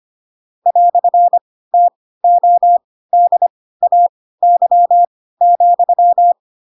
Morse code, heard as a keyed tone, about 25 wpm, spaced as for 16 wpm, 700 Hz. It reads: "TODAY,
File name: Pinocchio / Pinocchio ch4 10 25 ik1hge